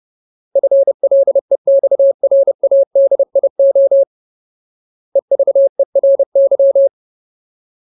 Morse code, heard as a keyed tone, about 30 wpm, 550 Hz. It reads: FLEXRADIO EVERY
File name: Ultimate Word List / 30WPM Ultimate